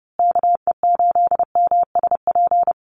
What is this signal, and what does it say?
Morse code, heard as a keyed tone, about 30 wpm, 700 Hz. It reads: KE8MSP
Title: USA Call Signs Repeated 2x 30wpm